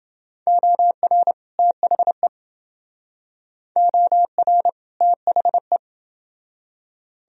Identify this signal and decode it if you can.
Morse code, heard as a keyed tone, about 28 wpm, 700 Hz. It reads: OR THE OR THE